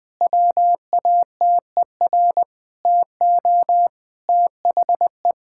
Morse code, heard as a keyed tone, about 20 wpm, 700 Hz. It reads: WATER TO THE
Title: Sentences from Top 1600 Words 20wpm